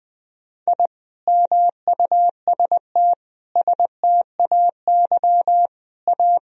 Morse code, heard as a keyed tone, about 19 wpm, 700 Hz. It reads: I MUST STAY A
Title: Sentences from Top 1000 Words - Review Part 1 - 20wpm